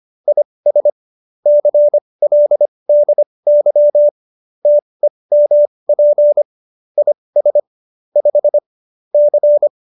Morse code, heard as a keyed tone, about 25 wpm, spaced as for 20 wpm, 590 Hz. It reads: IS CLDY TEMP IS 5 C